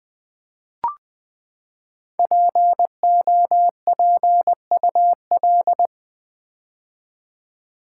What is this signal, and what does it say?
Morse code, heard as a keyed tone, about 20 wpm, 700 Hz. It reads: POPUL
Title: Build A Word - 5 Letters At A Time - 10-Letter Words - 20wpm